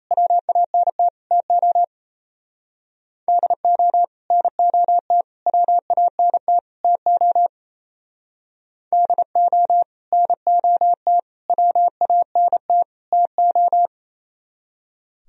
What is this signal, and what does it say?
Morse code, heard as a keyed tone, about 35 wpm, 700 Hz. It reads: WANT TO DO NOT WANT TO DO NOT WANT TO
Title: Speed-Racing - Sets of 4 Words 28wpm